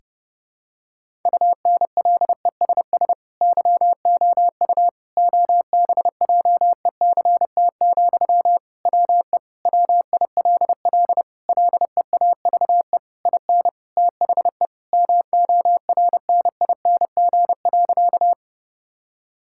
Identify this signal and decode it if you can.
Morse code, heard as a keyed tone, about 30 wpm, 700 Hz. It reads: UNLESS YOU OBJECT, WE WILL LEAVE IN THE MORNING.
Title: Sentences from Top 1000 Words - Review Part 6 - 30wpm